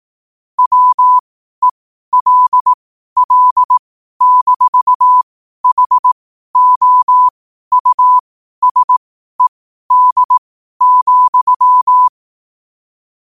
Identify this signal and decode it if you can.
Morse code, heard as a keyed tone, about 18 wpm, spaced as for 12 wpm, 1000 Hz. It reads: WELL-HOUSED,
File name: qotdfiles / 12wpm